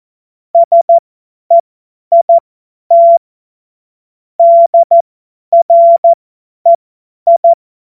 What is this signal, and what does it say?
Morse code, heard as a keyed tone, about 14 wpm, spaced as for 10 wpm, 680 Hz. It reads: SEIT DREI
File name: Witz-22-Juli-2025-50-F70